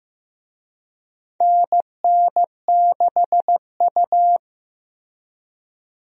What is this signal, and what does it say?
Morse code, heard as a keyed tone, about 15 wpm, 700 Hz. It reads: NN6U
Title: USA Call Signs - 4 Characters - 15wpm